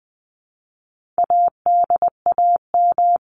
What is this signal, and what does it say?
Morse code, heard as a keyed tone, about 20 wpm, 700 Hz. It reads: ADAM
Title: CWT Exchange 20wpm